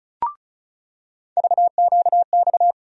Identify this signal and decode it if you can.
Morse code, heard as a keyed tone, about 34 wpm, 700 Hz. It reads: VQX
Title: USA Call Sign Postfixes - 3 Letters 35wpm